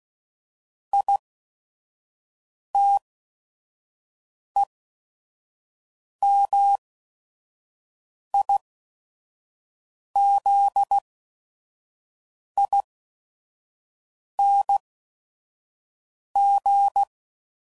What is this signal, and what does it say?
Morse code, heard as a keyed tone, about 16 wpm, spaced as for 5 wpm, 800 Hz. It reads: ITEMIZING
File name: Suffixes / Special ING's / VerySlowToFaster ING's